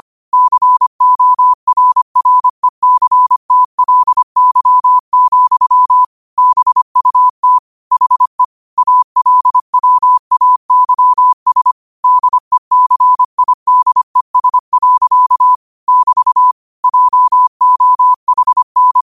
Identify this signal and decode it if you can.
Morse code, heard as a keyed tone, about 25 wpm, 1000 Hz. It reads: CORRECTLY, BUT HE ALWAYS DECIDES. <BT> JOHN